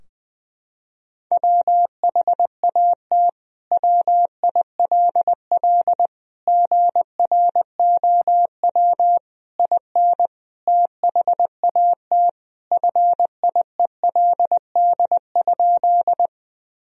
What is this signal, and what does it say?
Morse code, heard as a keyed tone, about 21 wpm, 700 Hz. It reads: WHAT WILL GROW IN THAT FIELD?
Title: Sentences from Top 1000 Words - Review Part 2 - 20wpm